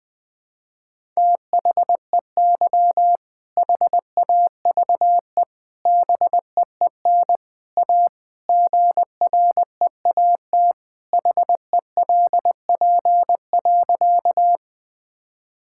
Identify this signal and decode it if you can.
Morse code, heard as a keyed tone, about 20 wpm, 700 Hz. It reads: THEY HAVE BEEN A GREAT HELP.
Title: Sentences with 3-Word Phrases - 20wpm